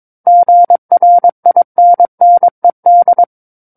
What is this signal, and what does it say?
Morse code, heard as a keyed tone, about 22 wpm, 700 Hz. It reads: GRINNED
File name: 06-Chapter-06 22 WPM